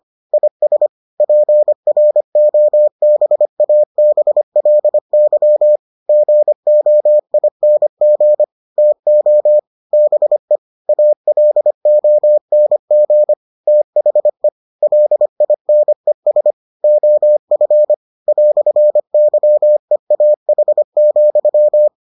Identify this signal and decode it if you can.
Morse code, heard as a keyed tone, about 25 wpm, 600 Hz. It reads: IS PROBABLY GOING TO BE ALONG THE LINES OF "YEAH,